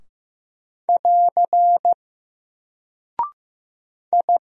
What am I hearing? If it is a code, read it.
Morse code, heard as a keyed tone, about 15 wpm, 700 Hz. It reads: <AR> I
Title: Single Character (Letter:Number:Punctuation:Prosign) Alt 15wpm